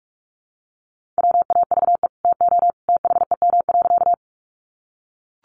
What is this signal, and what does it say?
Morse code, heard as a keyed tone, about 44 wpm, 700 Hz. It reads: WAVE TO THEM.
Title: Sentences from Top 1400 Words 45wpm